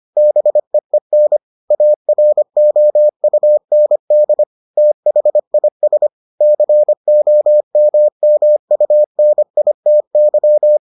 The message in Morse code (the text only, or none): BEEN AROUND THIS COMMUNITY